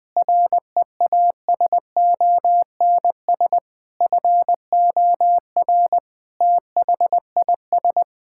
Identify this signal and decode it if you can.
Morse code, heard as a keyed tone, about 21 wpm, 700 Hz. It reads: REASONS FOR THIS
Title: Sentences with 8-Word Phrases - 20wpm